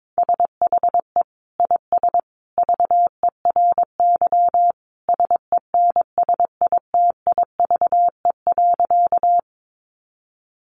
Morse code, heard as a keyed tone, about 22 wpm, 700 Hz. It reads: SHE IS VERY SENSITIVE.
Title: Sentences from Top 1400 Words 22wpm